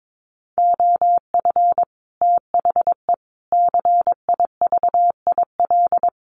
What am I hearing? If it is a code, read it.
Morse code, heard as a keyed tone, about 22 wpm, 700 Hz. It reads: OF THE CIVIL
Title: Sets of 5 Words Encore2 22wpm